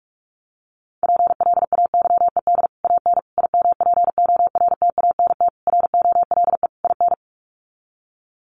Morse code, heard as a keyed tone, about 45 wpm, 690 Hz. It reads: PLAYED AN IMPORTANT ROLE IN
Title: Sets of 5 Words Encore2 45wpm